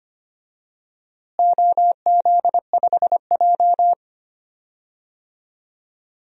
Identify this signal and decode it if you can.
Morse code, heard as a keyed tone, about 25 wpm, 700 Hz. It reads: OZ5J
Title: DXCC-100 Call Signs - 4 Characters - Repeated 2x - 25wpm